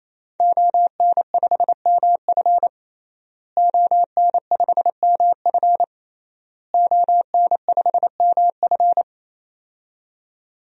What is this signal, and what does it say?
Morse code, heard as a keyed tone, about 28 wpm, 700 Hz. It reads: ON5MF ON5MF ON5MF
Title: DXCC-100 Call Signs - 5 Characters - Repeated 3x - 28wpm